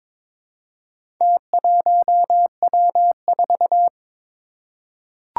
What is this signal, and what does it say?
Morse code, heard as a keyed tone, about 22 wpm, 700 Hz. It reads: T1W4
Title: Four Challenging Letter-Number Combinations 22wpm